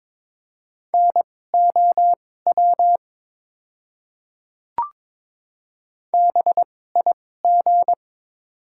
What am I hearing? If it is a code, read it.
Morse code, heard as a keyed tone, about 23 wpm, spaced as for 16 wpm, 700 Hz. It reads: NOW BIG
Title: 2x Character Spacing - Top 200 Words - 22wpm